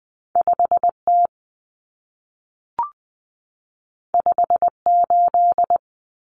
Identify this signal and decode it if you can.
Morse code, heard as a keyed tone, about 20 wpm, 700 Hz. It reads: 5T 58
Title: Common Cut Numbers - 2-Digit 20wpm